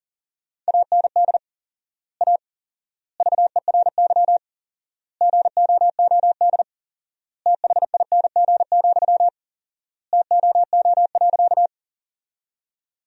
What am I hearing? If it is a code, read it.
Morse code, heard as a keyed tone, about 37 wpm, 700 Hz. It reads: AND A VERY GOOD THING, TOO.
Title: 4x Word-Spacing - Sentences from Top 200 Words - 40wpm